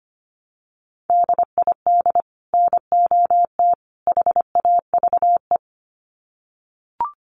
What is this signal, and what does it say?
Morse code, heard as a keyed tone, about 25 wpm, 700 Hz. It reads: DID NOT HAVE
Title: Sets of 3 Words 25wpm